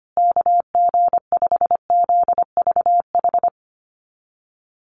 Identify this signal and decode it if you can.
Morse code, heard as a keyed tone, about 25 wpm, 700 Hz. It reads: KG5ZVH
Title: USA Call Signs Repeated 2x 25wpm